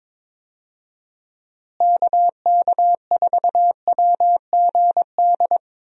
Morse code, heard as a keyed tone, about 22 wpm, 700 Hz. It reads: KK4WGD
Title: USA Call Signs - 6 Characters - Repeated 3x - 22wpm